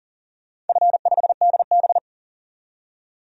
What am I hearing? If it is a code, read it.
Morse code, heard as a keyed tone, about 38 wpm, 700 Hz. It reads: F5DB
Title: DXCC-100 Call Signs - 4 Characters - 40wpm